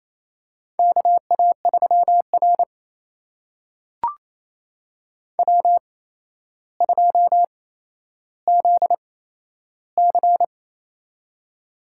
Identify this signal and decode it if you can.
Morse code, heard as a keyed tone, about 28 wpm, spaced as for 12 wpm, 700 Hz. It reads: KA3R W2ZC
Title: USA Call Signs - 4 Characters - 8x Character Spacing - 28wpm